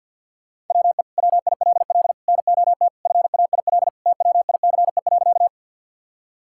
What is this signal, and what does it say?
Morse code, heard as a keyed tone, about 50 wpm, 700 Hz. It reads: WE WILL NOT FAIL TWICE.
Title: Sentences from Top 1600 Words 50wpm